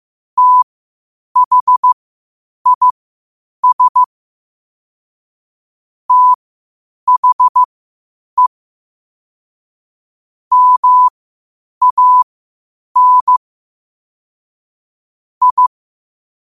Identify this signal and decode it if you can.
Morse code, heard as a keyed tone, about 15 wpm, spaced as for 8 wpm, 1000 Hz. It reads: THIS THE MAN I
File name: qotdfiles / 8wpm